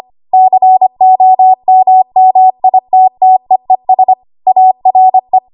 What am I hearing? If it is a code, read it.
Morse code, heard as a keyed tone, about 25 wpm, 750 Hz. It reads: COMMITTEES ARE
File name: Morse / 25WPM